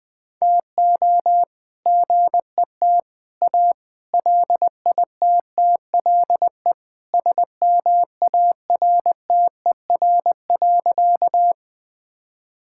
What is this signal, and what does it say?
Morse code, heard as a keyed tone, about 20 wpm, 700 Hz. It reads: TO GET A LITTLE SMARTER.
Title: All-Age Appropriate Jokes - 20wpm